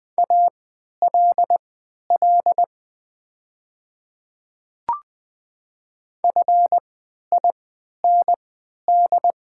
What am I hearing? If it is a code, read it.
Morse code, heard as a keyed tone, about 20 wpm, spaced as for 11 wpm, 700 Hz. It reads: ALL FIND
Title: 3x Character Spacing - Top 100 Words - 20wpm